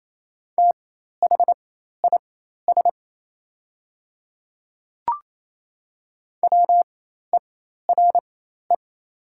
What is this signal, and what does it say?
Morse code, heard as a keyed tone, about 28 wpm, spaced as for 13 wpm, 690 Hz. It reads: THIS WERE
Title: 4x Character Spacing - Top 100 Words - 28wpm